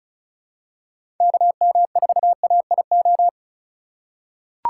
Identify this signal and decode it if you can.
Morse code, heard as a keyed tone, about 34 wpm, 700 Hz. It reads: KM4AIO
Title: USA Call Signs - 6 Characters - 2x Character Spacing - 35wpm